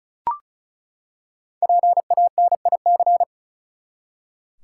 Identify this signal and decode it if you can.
Morse code, heard as a keyed tone, about 34 wpm, 700 Hz. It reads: PANIC
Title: Five Letter Words - 35wpm